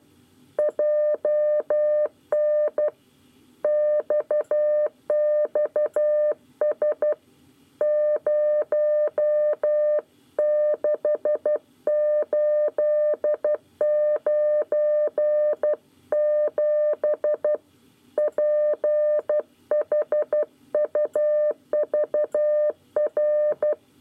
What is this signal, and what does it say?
Morse code, heard as a keyed tone, about 12 wpm, 600 Hz. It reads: JN XXS 06897 PHUVR
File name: 9 lezione cw 1